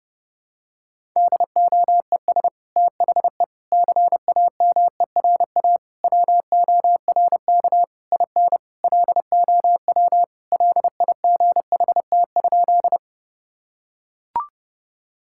Morse code, heard as a keyed tone, about 30 wpm, 700 Hz. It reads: DOES THE CAMERA WORK IN LOW LIGHT?
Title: Sentences from Top 1000 Words - Review Part 5 - 30wpm